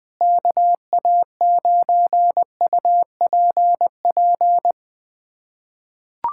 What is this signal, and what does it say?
Morse code, heard as a keyed tone, about 20 wpm, 700 Hz. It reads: KA9UPP
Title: USA Callsign DE Callsign 20wpm